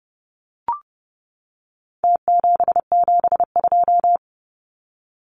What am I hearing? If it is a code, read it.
Morse code, heard as a keyed tone, about 30 wpm, 700 Hz. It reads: T772
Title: Numbers - Contest Style - 4-Digit 30wpm